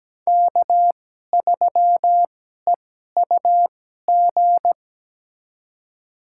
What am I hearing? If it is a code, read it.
Morse code, heard as a keyed tone, about 17 wpm, spaced as for 12 wpm, 700 Hz. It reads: K3EUG
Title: USA Call Signs - 5 Characters - 2x Character Spacing - 17wpm